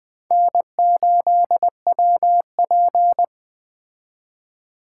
Morse code, heard as a keyed tone, about 20 wpm, 700 Hz. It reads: N8WP
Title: Speed Racing - USA 1x2 Call Signs 17wpm